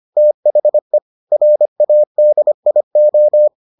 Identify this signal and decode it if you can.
Morse code, heard as a keyed tone, about 25 wpm, 600 Hz. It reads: THE RADIO